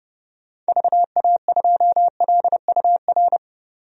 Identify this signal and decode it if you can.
Morse code, heard as a keyed tone, about 31 wpm, 700 Hz. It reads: VA2LUR
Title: DXCC-100 Call Signs - 6 Characters - 30wpm